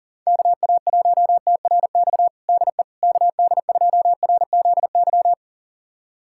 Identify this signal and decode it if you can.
Morse code, heard as a keyed tone, about 40 wpm, 700 Hz. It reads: KA1TRX DE KD2RZY
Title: USA Callsign DE Callsign - 6 Characters - 40wpm